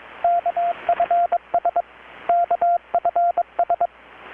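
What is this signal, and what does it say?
Morse code, heard as a keyed tone, about 22 wpm, 690 Hz, with radio noise. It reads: KFS KFS
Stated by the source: Morse from KPH's Night of Nights XX maritime radiotelegraphy event, received in Chiapas, Mexico about 3,800 km away on a Kenwood R-600 with a grounded 10 m long-wire antenna.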